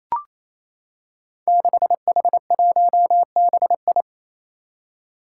Morse code, heard as a keyed tone, about 28 wpm, 700 Hz. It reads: 6H1BI
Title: Five Challenging Letter-Number Combinations 28wpm